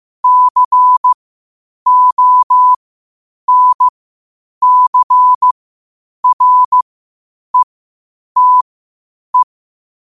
Morse code, heard as a keyed tone, about 15 wpm, spaced as for 8 wpm, 1000 Hz. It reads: CONCRETE
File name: qotdfiles / 8wpm